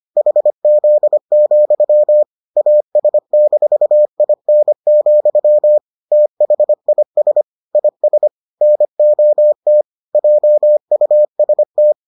Morse code, heard as a keyed tone, about 25 wpm, 600 Hz. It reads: HZ, AS-IN, THIS IS NOT JUST